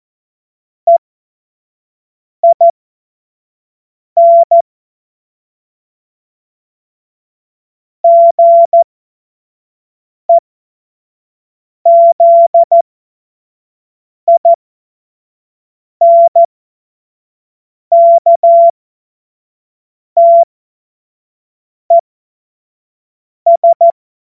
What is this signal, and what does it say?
Morse code, heard as a keyed tone, about 14 wpm, spaced as for 5 wpm, 680 Hz. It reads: EIN GEZINKTES
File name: Morsekurs / Witze / Witz 08. Februar 2022 25 F70